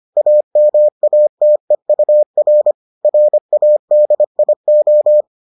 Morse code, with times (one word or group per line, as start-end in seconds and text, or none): 0.2-2.7 AMATEUR
3.0-5.2 RADIO